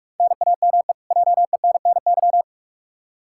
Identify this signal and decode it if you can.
Morse code, heard as a keyed tone, about 45 wpm, 700 Hz. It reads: NAME JENNY
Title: K1USN SST QSO Practice 45wpm